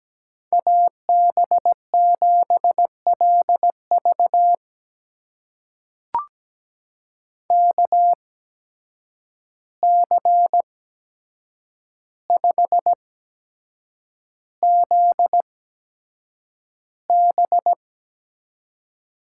Morse code, heard as a keyed tone, about 17 wpm, spaced as for 7 wpm, 700 Hz. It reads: AB7LV KC5ZB
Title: USA Call Signs - 5 Characters - 8x Character Spacing - 17wpm